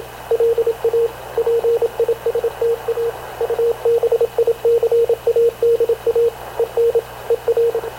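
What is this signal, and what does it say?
Morse code, heard as a keyed tone, about 28 wpm, 460 Hz, with radio noise. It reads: LA PISTA UBICADA EN ER
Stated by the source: Havana Radio CLA press (PX) broadcast, recorded by Radio Officer Paul Dumesnil aboard mv CNAV QUEST/CZDO.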